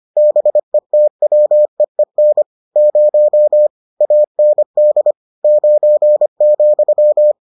Morse code, heard as a keyed tone, about 25 wpm, 600 Hz. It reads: BETWEEN 0 AND 9,